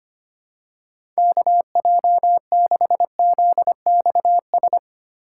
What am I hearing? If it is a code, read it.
Morse code, heard as a keyed tone, about 25 wpm, 700 Hz. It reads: KJ6ZXS
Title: USA Call Signs - 6 Characters - 8x Character Spacing - 25wpm